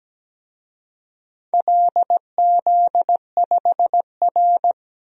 Morse code, heard as a keyed tone, about 17 wpm, 700 Hz. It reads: LZ5R
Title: DXCC-100 Call Signs - 4 Characters - 17wpm